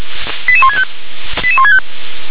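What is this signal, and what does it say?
Five-tone selective call (ZVEI), heard as quick runs of five tones, tones of about 70 ms. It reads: 88165 88165
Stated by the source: ZVEI five-tone selective call received on an SDR.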